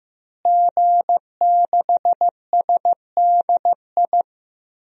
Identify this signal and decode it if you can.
Morse code, heard as a keyed tone, about 15 wpm, 700 Hz. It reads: G6SDI
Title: DXCC-100 Call Signs - 5 Characters - Repeated 2x - 15wpm